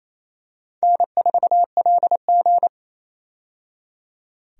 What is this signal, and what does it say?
Morse code, heard as a keyed tone, about 28 wpm, 700 Hz. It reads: N4LG
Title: USA Call Signs - 4 Characters - 28wpm